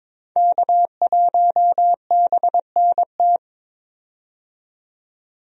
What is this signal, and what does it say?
Morse code, heard as a keyed tone, about 21 wpm, 700 Hz. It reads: K1BNT
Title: USA Call Signs - 5 Characters - 22wpm